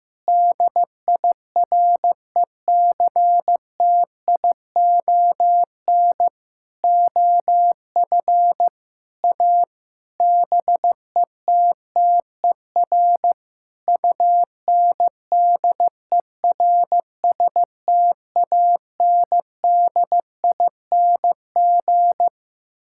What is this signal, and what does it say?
Morse code, heard as a keyed tone, about 16 wpm, 700 Hz. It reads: DIRECTION OF A BETTER UNDERSTANDING
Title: Sentences from Top 1000 Words - Review Part 5 - 15wpm